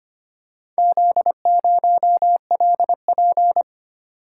Morse code, heard as a keyed tone, about 25 wpm, 700 Hz. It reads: Z0LP
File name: practice / 25wpm / Four Letter-Number Combinations - 25wpm